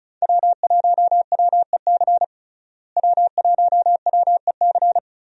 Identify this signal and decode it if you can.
Morse code, heard as a keyed tone, about 36 wpm, 700 Hz. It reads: W1WEC W1WEC
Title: USA Call Signs - 5 Characters - Repeated 3x - 35wpm